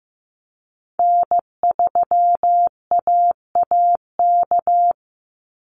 Morse code, heard as a keyed tone, about 15 wpm, 700 Hz. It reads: N3AAK
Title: Call Signs Repeated 3x - 15wpm